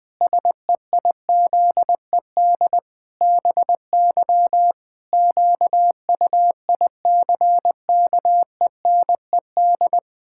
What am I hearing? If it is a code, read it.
Morse code, heard as a keyed tone, about 20 wpm, 700 Hz. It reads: SEIZED BY QUICKENED
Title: Pangram Challenge v2 - 20wpm